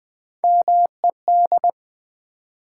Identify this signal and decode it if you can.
Morse code, heard as a keyed tone, about 20 wpm, 700 Hz. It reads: MED
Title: Build A Word - 1 Letter At A Time - 5-Letter Words - 20wpm